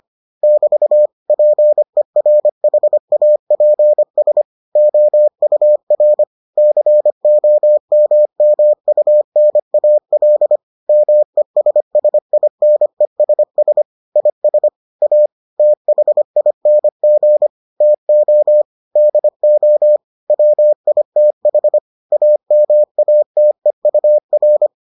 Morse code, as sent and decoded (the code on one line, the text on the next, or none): -...- / .--. .-. .... .- .--. ... / --- ..- .-. / -.-. --- -- -- ..- -. .- .-.. / -- . ... ... .. -. ... ... / .. ... / .- / - .... .. -. --. / - --- / -.. --- / .-- .. - .... / .- -- .- - . ..- .-.
<BT> PERHAPS OUR COMMUNAL MESSINESS IS A THING TO DO WITH AMATEUR